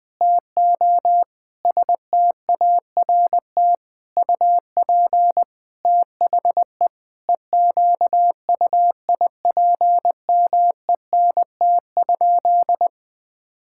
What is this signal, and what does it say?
Morse code, heard as a keyed tone, about 20 wpm, 700 Hz. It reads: TO START UP THE EQUIPMENT?
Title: Sentences from Top 1000 Words - Review Part 6 - 20wpm